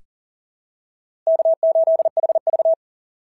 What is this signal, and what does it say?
Morse code, heard as a keyed tone, about 40 wpm, 650 Hz. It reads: K8HV
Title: SOTA QSO Practice 40wpm